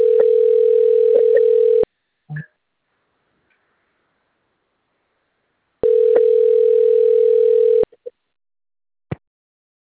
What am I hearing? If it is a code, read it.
Ringback tone (North America), 440 + 480 Hz.